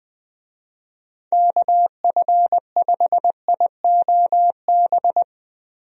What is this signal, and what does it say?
Morse code, heard as a keyed tone, about 21 wpm, 700 Hz. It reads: KF5IOB
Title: USA Call Signs - 6 Characters - Repeated 2x - 20wpm